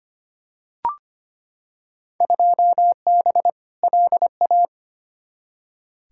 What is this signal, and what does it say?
Morse code, heard as a keyed tone, about 25 wpm, 700 Hz. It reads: 2B LA
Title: Field Day Exchanges Encore - 25wpm